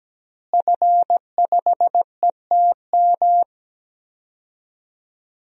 Morse code, heard as a keyed tone, about 17 wpm, 700 Hz. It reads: F5ETM
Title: DXCC-100 Call Signs - 5 Characters - Repeated 2x - 17wpm